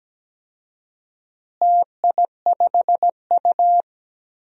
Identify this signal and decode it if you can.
Morse code, heard as a keyed tone, about 17 wpm, 700 Hz. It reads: TI5U
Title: DXCC-100 Call Signs - 4 Characters - Repeated 3x - 17wpm